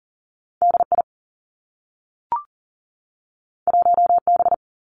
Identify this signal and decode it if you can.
Morse code, heard as a keyed tone, about 40 wpm, 700 Hz. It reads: D I 1 B
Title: Two Challenging Letter-Number Combinations 40wpm